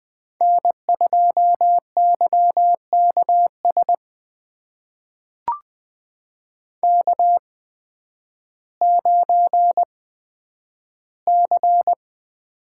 Morse code, heard as a keyed tone, about 20 wpm, spaced as for 11 wpm, 700 Hz. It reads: N2YKS K9C
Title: USA Call Signs - 5 Characters - 8x Character Spacing - 20wpm